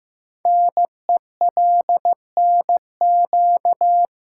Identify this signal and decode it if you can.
Morse code, heard as a keyed tone, about 15 wpm, 700 Hz. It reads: NELNQ
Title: Code Groups - 5 Letters - 15wpm